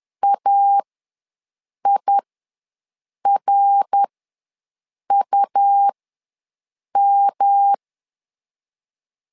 Morse code, heard as a keyed tone, about 10 wpm, spaced as for 6 wpm, 800 Hz. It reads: AIRUM
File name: LEZIONI / 3 MWRO / 3 MWRO ws 10wpm 2310